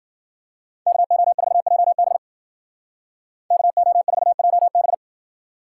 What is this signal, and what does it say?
Morse code, heard as a keyed tone, about 50 wpm, 700 Hz. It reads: KO4JB KO4JB